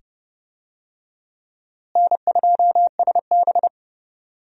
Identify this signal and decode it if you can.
Morse code, heard as a keyed tone, about 29 wpm, 700 Hz. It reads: N2SB
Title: USA Call Signs - 4 Characters - Repeated 2x - 30wpm